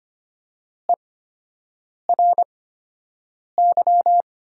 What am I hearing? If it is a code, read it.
Morse code, heard as a keyed tone, about 25 wpm, spaced as for 7 wpm, 700 Hz. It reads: ERY
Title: 8x Character Spacing - Top 200 Words - 25wpm